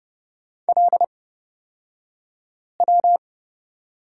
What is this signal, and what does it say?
Morse code, heard as a keyed tone, about 29 wpm, spaced as for 5 wpm, 690 Hz. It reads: LW
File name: Lesson 36d - BT - Two Characters 30wpm